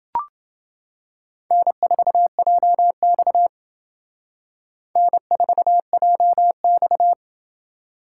Morse code, heard as a keyed tone, about 29 wpm, 700 Hz. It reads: N4JX N4JX